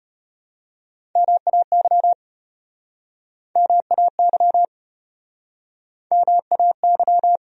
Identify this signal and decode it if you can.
Morse code, heard as a keyed tone, about 32 wpm, 700 Hz. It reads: MAY MAY MAY